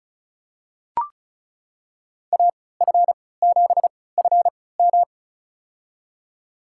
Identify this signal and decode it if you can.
Morse code, heard as a keyed tone, about 36 wpm, spaced as for 20 wpm, 700 Hz. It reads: AF7FM
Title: USA Call Signs - 3x Character Spacing - 35wpm